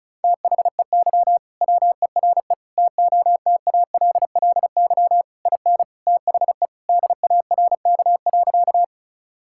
Morse code, heard as a keyed tone, about 36 wpm, 700 Hz. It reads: THEY WERE TOTALLY IN THE DARK.
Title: Sentences from Top 1600 Words 35wpm